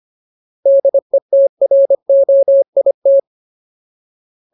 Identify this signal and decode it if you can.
Morse code, heard as a keyed tone, about 25 wpm, 550 Hz. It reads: DETROIT
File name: Ultimate Word List / 25WPM Ultimate